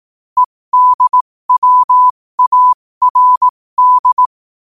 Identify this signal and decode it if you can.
Morse code, heard as a keyed tone, about 18 wpm, spaced as for 15 wpm, 1000 Hz. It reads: EDWARD